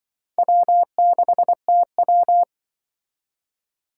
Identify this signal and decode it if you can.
Morse code, heard as a keyed tone, about 25 wpm, 700 Hz. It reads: W6TW
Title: Speed Racing - USA 1x2 Call Signs 20wpm